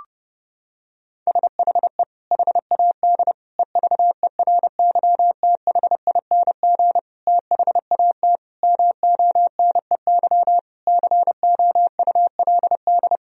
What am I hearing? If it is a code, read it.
Morse code, heard as a keyed tone, about 30 wpm, 700 Hz. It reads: SHE HAD EVERYTHING THAT MONEY COULD
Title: Sentences from Top 1000 Words - Review Part 4 - 30wpm